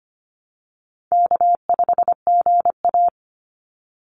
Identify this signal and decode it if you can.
Morse code, heard as a keyed tone, about 25 wpm, 700 Hz. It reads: K5GA
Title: Easy Call Signs - 25wpm